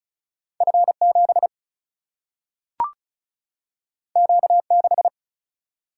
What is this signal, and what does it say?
Morse code, heard as a keyed tone, about 36 wpm, 700 Hz. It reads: F 7 Q 6
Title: Single-Letter Single-Number 35wpm